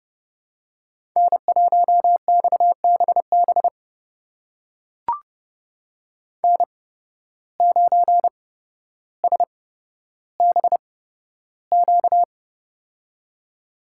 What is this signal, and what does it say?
Morse code, heard as a keyed tone, about 31 wpm, spaced as for 13 wpm, 700 Hz. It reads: N1XBB N9SBQ
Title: USA Call Signs - 5 Characters - 8x Character Spacing - 30wpm